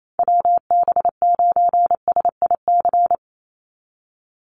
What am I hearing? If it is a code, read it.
Morse code, heard as a keyed tone, about 28 wpm, 700 Hz. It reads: WB9SIC
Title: USA Call Signs 28wpm